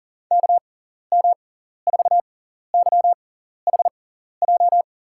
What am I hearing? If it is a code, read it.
Morse code, heard as a keyed tone, about 39 wpm, spaced as for 14 wpm, 700 Hz. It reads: KM4YHJ
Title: USA Call Signs - 6 Characters - 6x Character Spacing - 40wpm